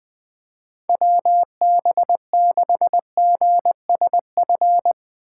Morse code, heard as a keyed tone, about 20 wpm, 700 Hz. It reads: WB6GSF